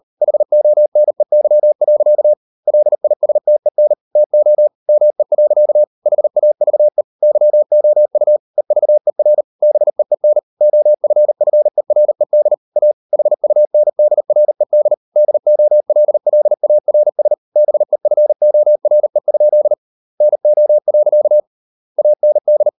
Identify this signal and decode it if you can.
Morse code, heard as a keyed tone, about 38 wpm, 600 Hz. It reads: HONEY. LISTEN TO ME. HAVE YOU EVER BEEN OFFERED A HUNDRED DOLLARS BEFORE? NO. AND